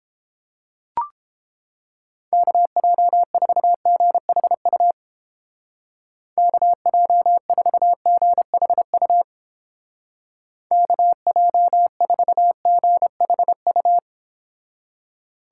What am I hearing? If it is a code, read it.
Morse code, heard as a keyed tone, about 29 wpm, 700 Hz. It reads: KJ4GHU KJ4GHU KJ4GHU